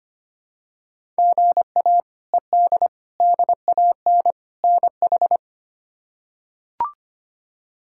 Morse code, heard as a keyed tone, about 25 wpm, 700 Hz. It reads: GA ED DAN NH
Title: K1USN SST QSO Practice - 25wpm